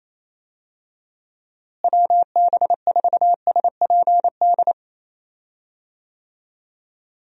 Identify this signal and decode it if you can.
Morse code, heard as a keyed tone, about 28 wpm, 700 Hz. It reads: WB4SPD
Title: USA Call Signs - 6 Characters - Repeated 2x - 28wpm